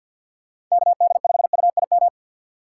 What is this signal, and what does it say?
Morse code, heard as a keyed tone, about 47 wpm, 700 Hz. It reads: KD5UIM
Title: USA Call Signs - 6 Characters - 4x Character Spacing - 50wpm